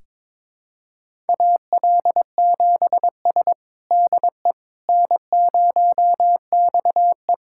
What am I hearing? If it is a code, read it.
Morse code, heard as a keyed tone, about 22 wpm, 700 Hz. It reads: AL7S DE N0XE
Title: USA Callsign DE Callsign - 4 Characters - 22wpm